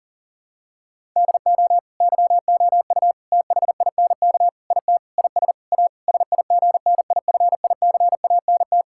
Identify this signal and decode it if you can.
Morse code, heard as a keyed tone, about 37 wpm, 700 Hz. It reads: DO YOU THINK IT IS A SIGNIFICANT
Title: Sentences from Top 1000 Words - Review Part 6 - 40wpm